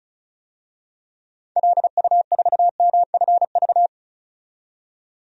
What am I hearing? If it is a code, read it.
Morse code, heard as a keyed tone, about 36 wpm, 690 Hz. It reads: LU4MFV
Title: DXCC-100 Call Signs - 6 Characters - 35wpm